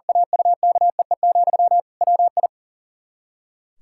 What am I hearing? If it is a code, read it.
Morse code, heard as a keyed tone, about 40 wpm, 700 Hz. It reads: AUKEE, WI
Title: Practice QSO - Exchange 1 - Beginning - 40wpm